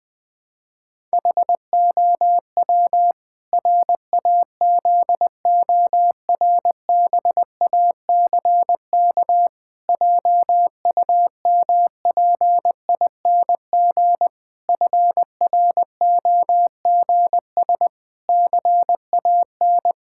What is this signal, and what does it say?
Morse code, heard as a keyed tone, about 21 wpm, 700 Hz. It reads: HOW RAZORBACK JUMPING FROGS CAN
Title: Pangram Challenge v2 - 20wpm